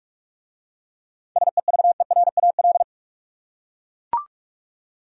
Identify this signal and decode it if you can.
Morse code, heard as a keyed tone, about 44 wpm, 700 Hz. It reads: SEVERAL